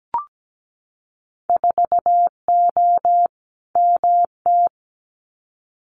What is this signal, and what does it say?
Morse code, heard as a keyed tone, about 17 wpm, 700 Hz. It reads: 4O MT
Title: Winter Field Day Exchanges Encore 17wpm